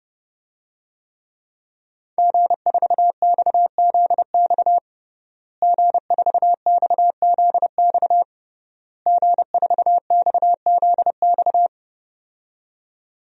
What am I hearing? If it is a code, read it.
Morse code, heard as a keyed tone, about 32 wpm, 700 Hz. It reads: G4XZX G4XZX G4XZX
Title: DXCC-100 Call Signs - 5 Characters - Repeated 3x - 30wpm